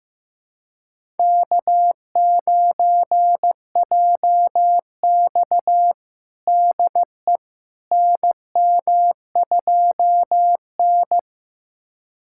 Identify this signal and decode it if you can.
Morse code, heard as a keyed tone, about 15 wpm, 700 Hz. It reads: K9JX DE NM2N
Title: USA Callsign DE Callsign - 4 Characters - 15wpm